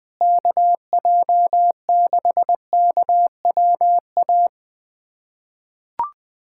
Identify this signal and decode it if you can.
Morse code, heard as a keyed tone, about 21 wpm, 700 Hz. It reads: KJ6KWA
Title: USA Callsign DE Callsign - 6 Characters - 20wpm